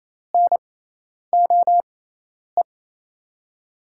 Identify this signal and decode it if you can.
Morse code, heard as a keyed tone, about 28 wpm, spaced as for 10 wpm, 700 Hz. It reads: NOE
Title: USA Call Signs - 6x Character Spacing - 28wpm